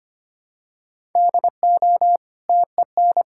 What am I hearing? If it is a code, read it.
Morse code, heard as a keyed tone, about 25 wpm, 700 Hz. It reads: DO TEN
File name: Easy Sentences with Hints - 25wpm